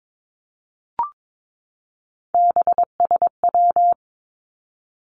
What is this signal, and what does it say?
Morse code, heard as a keyed tone, about 22 wpm, 700 Hz. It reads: BSW
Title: Three Challenging Letter-Number Combinations 22wpm